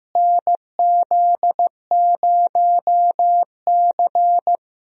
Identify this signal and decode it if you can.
Morse code, heard as a keyed tone, about 16 wpm, 700 Hz. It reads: NZ0C
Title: USA Call Signs - 4 Characters - Repeated 2x - 15wpm